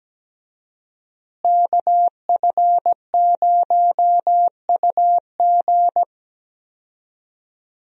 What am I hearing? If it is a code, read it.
Morse code, heard as a keyed tone, about 17 wpm, 700 Hz. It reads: KF0UG
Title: USA Call Signs - 5 Characters - Repeated 3x - 17wpm